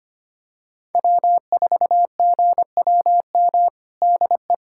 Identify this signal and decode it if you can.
Morse code, heard as a keyed tone, about 25 wpm, 700 Hz. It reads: W4GWM DE
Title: USA Callsign DE Callsign - 5 Characters - 25wpm